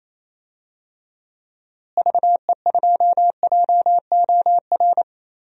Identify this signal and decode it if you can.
Morse code, heard as a keyed tone, about 28 wpm, 700 Hz. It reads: VE2JOR
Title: DXCC-100 Call Signs - 6 Characters - Repeated 2x - 28wpm